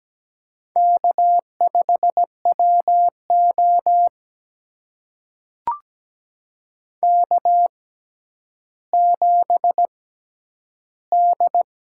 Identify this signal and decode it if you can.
Morse code, heard as a keyed tone, about 17 wpm, spaced as for 10 wpm, 700 Hz. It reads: K5WO K7D
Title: USA Call Signs - 4 Characters - 6x Character Spacing - 17wpm